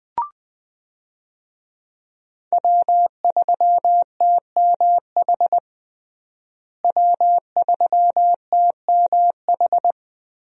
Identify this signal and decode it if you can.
Morse code, heard as a keyed tone, about 21 wpm, 700 Hz. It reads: W3TMH W3TMH
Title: USA Call Signs - 5 Characters - Repeated 2x - 20wpm